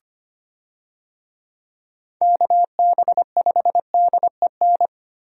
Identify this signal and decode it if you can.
Morse code, heard as a keyed tone, about 25 wpm, 700 Hz. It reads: KB5DEN
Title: USA Call Signs - 6 Characters - Repeated 2x - 25wpm